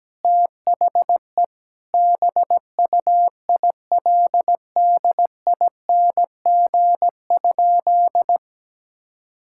Morse code, heard as a keyed tone, about 17 wpm, 700 Hz. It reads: THE BUILDING?
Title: Sentences from Top 1000 Words - Review Part 6 - 17wpm